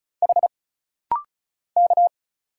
Morse code, heard as a keyed tone, about 34 wpm, 700 Hz. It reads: H K
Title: Single Letter-Number Alt - Rapid-Fire 35wpm